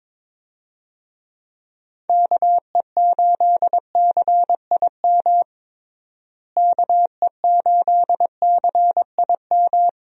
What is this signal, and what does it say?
Morse code, heard as a keyed tone, about 22 wpm, 700 Hz. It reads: KE8CIM KE8CIM
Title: USA Call Signs - 6 Characters - Repeated 3x - 22wpm